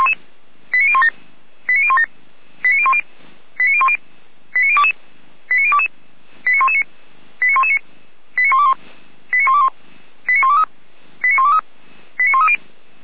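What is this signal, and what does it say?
Five-tone selective call (ZVEI), heard as quick runs of five tones, tones of about 70 ms. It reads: 89017 89017 89010 89010 89022 89022 89119 89119 89121 89121 89123 89123 89130
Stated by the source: ZVEI five-tone call received on an SDR.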